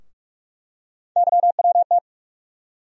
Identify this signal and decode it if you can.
Morse code, heard as a keyed tone, about 46 wpm, 700 Hz. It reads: YWT
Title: USA Call Sign Postfixes - 3 Letters 45wpm